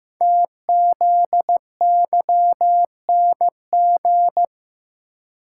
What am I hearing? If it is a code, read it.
Morse code, heard as a keyed tone, about 15 wpm, 700 Hz. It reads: TZYNG
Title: Code Groups - 5 Letter:Numbers - 15wpm